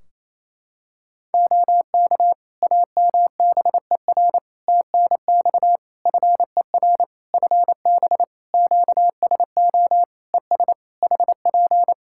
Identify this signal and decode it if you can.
Morse code, heard as a keyed tone, about 27 wpm, 700 Hz. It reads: OK AMBER TNX FER FB QSO ES HP
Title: Practice QSO - Exchange 4 - Ending - 28wpm